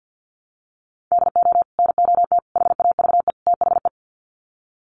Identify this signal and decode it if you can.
Morse code, heard as a keyed tone, about 50 wpm, 700 Hz. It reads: DO NOT HAVE THE
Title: Sets of 4 Words 50wpm